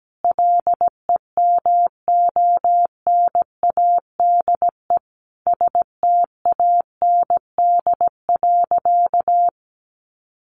Morse code, heard as a keyed tone, about 17 wpm, 700 Hz. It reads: LEMONADE STAND.
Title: Additional Easy Phrases 17wpm